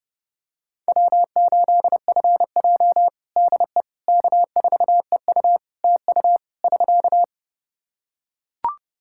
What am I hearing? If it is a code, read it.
Morse code, heard as a keyed tone, about 30 wpm, 700 Hz. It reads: W8FJ DE K4EU TU <SK>
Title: Practice QSO - Exchange 4 - Ending - 30wpm